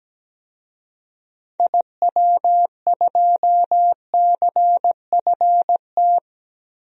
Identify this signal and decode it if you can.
Morse code, heard as a keyed tone, about 17 wpm, 700 Hz. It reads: IW2CFT
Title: DXCC-100 Call Signs - 6 Characters - 17wpm